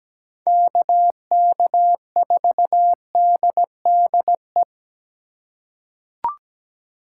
Morse code, heard as a keyed tone, about 17 wpm, 700 Hz. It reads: KK4DDE
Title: USA Call Signs - 6 Characters - 4x Character Spacing - 17wpm